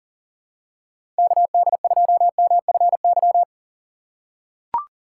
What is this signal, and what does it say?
Morse code, heard as a keyed tone, about 40 wpm, 700 Hz. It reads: KD2MFY
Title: USA Call Signs - 6 Characters - 2x Character Spacing - 40wpm